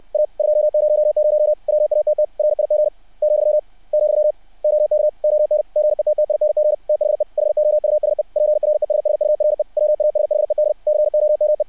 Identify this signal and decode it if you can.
Morse code, heard as a keyed tone, about 60 wpm, 600 Hz. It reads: A 200 WATT RIG , , OK ON GETTING THE HOUSE PRESSURE WASHED COME
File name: archived / CRSnet